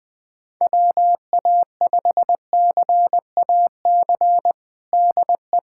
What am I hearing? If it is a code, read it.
Morse code, heard as a keyed tone, about 20 wpm, 700 Hz. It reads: WA5CAC DE
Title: USA Callsign DE Callsign 20wpm